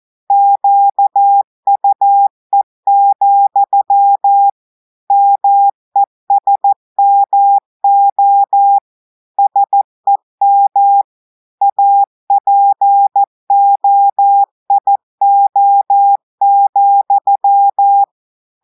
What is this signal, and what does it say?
Morse code, heard as a keyed tone, about 14 wpm, 800 Hz. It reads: QUE, MESMO SEM APOIO,
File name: audios / portugais / portugais texte en clair 1 vitesse 14 mots minutes